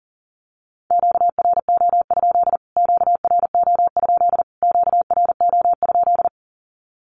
Morse code, heard as a keyed tone, about 40 wpm, 700 Hz. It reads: QRO? QRO? QRO?
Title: Q-Codes Repeated 3x 40wpm